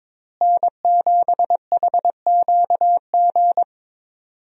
Morse code, heard as a keyed tone, about 22 wpm, 700 Hz. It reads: N7HQG
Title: USA Call Signs - 5 Characters - Repeated 3x - 22wpm